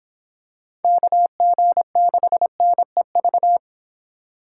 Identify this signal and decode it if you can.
Morse code, heard as a keyed tone, about 26 wpm, 700 Hz. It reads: KG6NEV